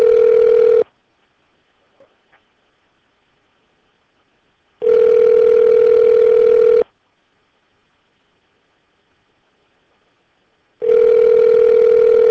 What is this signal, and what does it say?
Ringback tone (North America), 440 + 480 Hz.